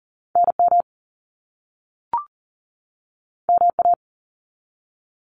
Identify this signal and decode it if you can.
Morse code, heard as a keyed tone, about 39 wpm, 700 Hz. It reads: NM MA
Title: US States 40wpm